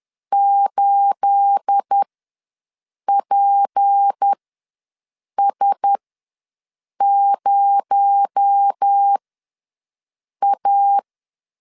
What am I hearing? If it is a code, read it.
Morse code, heard as a keyed tone, about 10 wpm, spaced as for 6 wpm, 800 Hz. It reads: 8PS0A
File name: LEZIONI / 9 28 ws 10wpm 4124